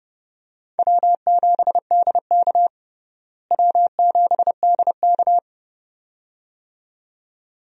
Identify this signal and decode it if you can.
Morse code, heard as a keyed tone, about 30 wpm, 700 Hz. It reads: W7DK W7DK
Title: USA Call Signs - 4 Characters - Repeated 2x - 30wpm